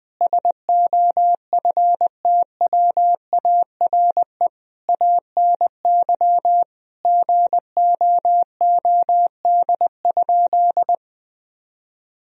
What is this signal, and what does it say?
Morse code, heard as a keyed tone, about 20 wpm, 700 Hz. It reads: SOFTWARE ANY GOOD?
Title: Sentences from Top 1000 Words - Review Part 3 - 20wpm